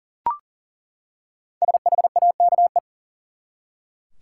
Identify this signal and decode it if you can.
Morse code, heard as a keyed tone, about 38 wpm, 700 Hz. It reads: SHAKE
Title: Five Letter Words - 40wpm